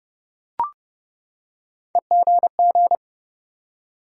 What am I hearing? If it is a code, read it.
Morse code, heard as a keyed tone, about 30 wpm, 700 Hz. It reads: EGG
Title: Three Letter Words - 30wpm